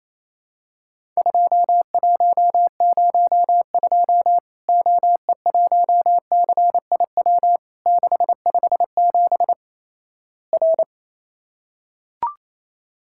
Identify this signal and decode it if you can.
Morse code, heard as a keyed tone, about 28 wpm, 700 Hz. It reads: 2102 OE1CIW 657 R
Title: Worked All Europe DX Contest - QTC Exchange - 28wpm